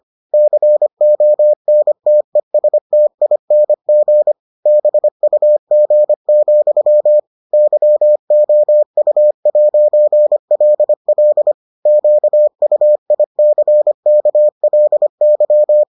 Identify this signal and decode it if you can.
Morse code, heard as a keyed tone, about 25 wpm, 600 Hz. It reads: CONTESTING BUG, YOU'LL QUICKLY